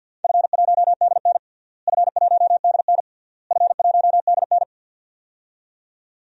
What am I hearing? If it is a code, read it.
Morse code, heard as a keyed tone, about 52 wpm, 700 Hz. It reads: F1DN F1DN F1DN
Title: DXCC-100 Call Signs - 4 Characters - Repeated 3x - 50wpm